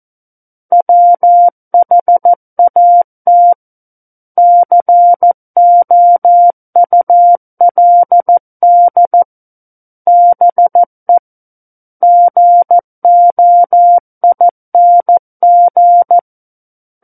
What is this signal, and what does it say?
Morse code, heard as a keyed tone, about 14 wpm, 700 Hz. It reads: WHAT COULD BE GOING
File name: Book One / 05-Chapter 5 - 14 WPM